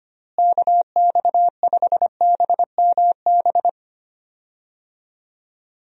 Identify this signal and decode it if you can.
Morse code, heard as a keyed tone, about 25 wpm, 700 Hz. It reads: KX5BMB
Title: USA Call Signs - 6 Characters - Repeated 2x - 25wpm